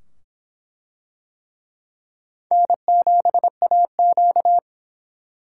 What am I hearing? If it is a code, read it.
Morse code, heard as a keyed tone, about 23 wpm, 700 Hz. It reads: N7AQ